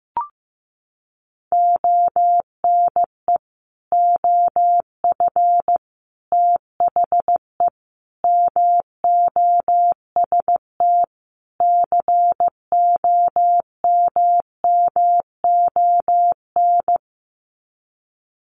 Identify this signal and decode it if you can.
Morse code, heard as a keyed tone, about 15 wpm, 700 Hz. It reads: ONE OF THE MOST COMMON
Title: Sets of 5 Words Encore2 15wpm